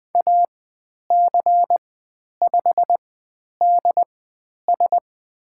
Morse code, heard as a keyed tone, about 20 wpm, spaced as for 10 wpm, 700 Hz. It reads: AC5DS
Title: 102 Lesson 17g - Letter u - Call Signs 20wpm